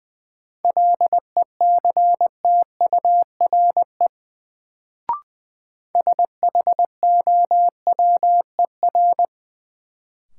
Morse code, heard as a keyed tone, about 20 wpm, 700 Hz. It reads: LECTURE SHOWER
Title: Sentences from Top 1600 Words - New Word Drill Alt - 20wpm